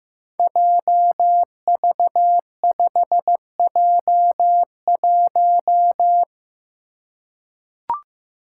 Morse code, heard as a keyed tone, about 15 wpm, 700 Hz. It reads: JV5J1
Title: Five Challenging Letter-Number Combinations 15wpm